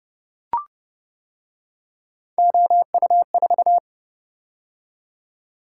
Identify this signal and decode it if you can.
Morse code, heard as a keyed tone, about 29 wpm, 700 Hz. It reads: OU4